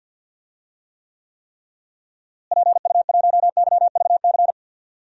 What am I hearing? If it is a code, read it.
Morse code, heard as a keyed tone, about 52 wpm, 700 Hz. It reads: PU1YVC
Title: DXCC-100 Call Signs - 6 Characters - Repeated 3x - 50wpm